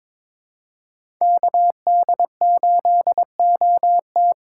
Morse code, heard as a keyed tone, about 22 wpm, 700 Hz. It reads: KD8OT
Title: USA Call Signs - 5 Characters - 3x Character Spacing - 22wpm